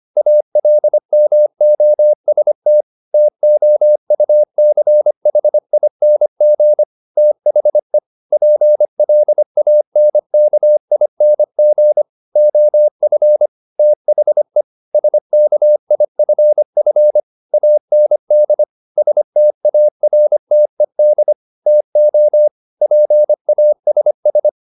Morse code, heard as a keyed tone, about 25 wpm, 600 Hz. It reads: ALMOST TOUCHING THE PLANKING OF THE SKIFF AND STARTED TO PASS